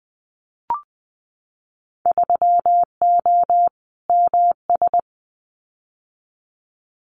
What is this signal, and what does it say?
Morse code, heard as a keyed tone, about 20 wpm, 700 Hz. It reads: 3O MS
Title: Winter Field Day Exchanges Encore 20wpm